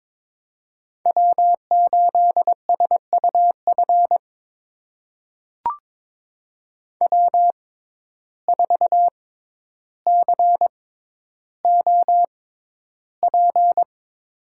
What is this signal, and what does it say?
Morse code, heard as a keyed tone, about 22 wpm, spaced as for 11 wpm, 700 Hz. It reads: W8SUF W4COP
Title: USA Call Signs - 5 Characters - 6x Character Spacing - 22wpm